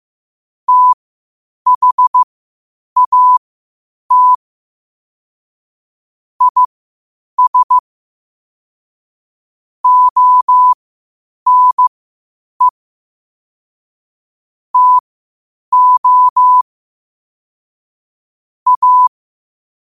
Morse code, heard as a keyed tone, about 15 wpm, spaced as for 8 wpm, 1000 Hz. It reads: THAT IS ONE TO A
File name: qotdfiles / 8wpm